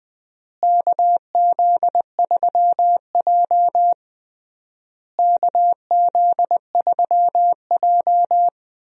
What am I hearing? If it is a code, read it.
Morse code, heard as a keyed tone, about 21 wpm, 700 Hz. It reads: KZ3J KZ3J
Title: USA Call Signs - 4 Characters - Repeated 3x - 20wpm